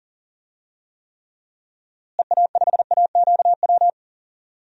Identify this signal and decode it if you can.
Morse code, heard as a keyed tone, about 40 wpm, 700 Hz. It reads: EA5AQW
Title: DXCC-100 Call Signs - 6 Characters - Repeated 3x - 40wpm